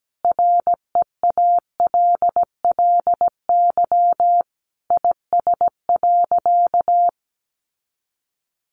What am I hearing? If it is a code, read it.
Morse code, heard as a keyed tone, about 17 wpm, 700 Hz. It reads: REALLY IS.
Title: Sentences from Top 600 Words 17wpm